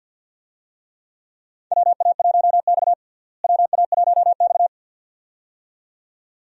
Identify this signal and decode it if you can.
Morse code, heard as a keyed tone, about 52 wpm, 700 Hz. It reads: WA1X WA1X
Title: USA Call Signs - 4 Characters - Repeated 2x - 50wpm